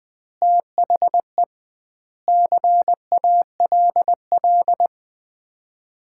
Morse code, heard as a keyed tone, about 20 wpm, 700 Hz. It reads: THE CALL
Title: Common QSO Phrases - 2x Word-Spacing - 20wpm